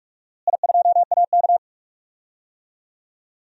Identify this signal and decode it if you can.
Morse code, heard as a keyed tone, about 44 wpm, 690 Hz. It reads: I2AK
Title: DXCC-100 Call Signs - 4 Characters - Repeated 3x - 45wpm